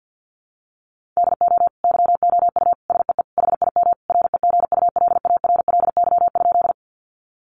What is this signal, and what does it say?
Morse code, heard as a keyed tone, about 50 wpm, 700 Hz. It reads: DO YOU SEE HIM REGULARLY?
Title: Sentences from Top 1400 Words 50wpm